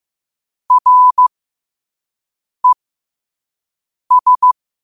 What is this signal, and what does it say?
Morse code, heard as a keyed tone, about 15 wpm, spaced as for 5 wpm, 1000 Hz. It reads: RES
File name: qotdfiles / 5wpm